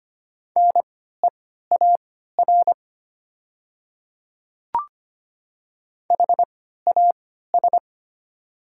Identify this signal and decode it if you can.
Morse code, heard as a keyed tone, about 25 wpm, spaced as for 14 wpm, 700 Hz. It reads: NEAR HAS
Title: 3x Character Spacing - Top 200 Words - 25wpm